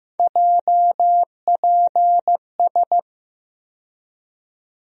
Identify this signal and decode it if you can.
Morse code, heard as a keyed tone, about 15 wpm, 700 Hz. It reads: JPS
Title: USA Call Sign Postfixes - 3 Letters 15wpm